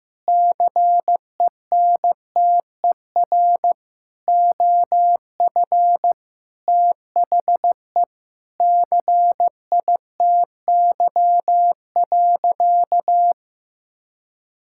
Morse code, heard as a keyed tone, about 15 wpm, 700 Hz. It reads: CENTER OF THE CITY.
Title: Sentences from Top 1000 Words - Review Part 6 - 15wpm